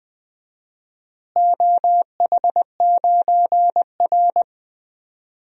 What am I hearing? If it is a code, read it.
Morse code, heard as a keyed tone, about 21 wpm, 700 Hz. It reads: OH9R
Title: DXCC-100 Call Signs - 4 Characters - Repeated 3x - 20wpm